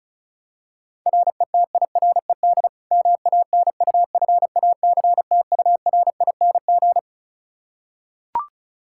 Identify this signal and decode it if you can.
Morse code, heard as a keyed tone, about 36 wpm, 700 Hz. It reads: RETIRED MANUFACTURING
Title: Practice QSO - Exchange 3 - More Info - 35wpm